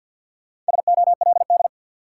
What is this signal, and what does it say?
Morse code, heard as a keyed tone, about 53 wpm, 700 Hz. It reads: SOLD
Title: Sentences from Top 1400 Words - New Word Drill - 50wpm